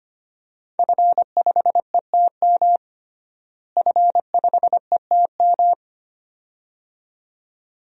Morse code, heard as a keyed tone, about 25 wpm, 700 Hz. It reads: F5ETM F5ETM
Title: DXCC-100 Call Signs - 5 Characters - Repeated 2x - 25wpm